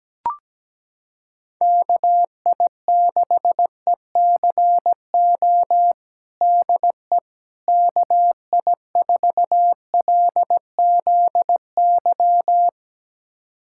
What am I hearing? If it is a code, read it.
Morse code, heard as a keyed tone, about 17 wpm, 700 Hz. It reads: KI6ECO DE KI4LZY
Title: USA Callsign DE Callsign - 6 Characters - 17wpm